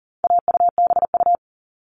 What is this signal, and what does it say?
Morse code, heard as a keyed tone, about 40 wpm, 700 Hz. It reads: AUBU
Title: All Cut Numbers - 4-Digit 40wpm